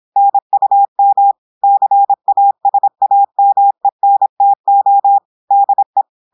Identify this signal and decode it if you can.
Morse code, heard as a keyed tone, about 26 wpm, 800 Hz. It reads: NUM CASAMENTO DE
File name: audios / portugais / portugais texte en clair 1 vitesse 26 mots minutes